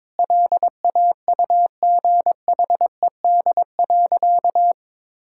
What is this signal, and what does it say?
Morse code, heard as a keyed tone, about 22 wpm, 700 Hz. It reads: LAUGHED.
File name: practice / 22wpm / Sentences with 2-Word Phrases - 22wpm